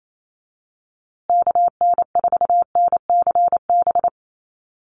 Morse code, heard as a keyed tone, about 28 wpm, 700 Hz. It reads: KN4NCB
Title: USA Call Signs Repeated 3x 28wpm